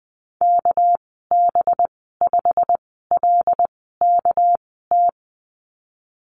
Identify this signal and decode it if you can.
Morse code, heard as a keyed tone, about 20 wpm, spaced as for 14 wpm, 700 Hz. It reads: KB5LKT
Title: 213 Lesson 34g - Number 8 - Call Signs 20wpm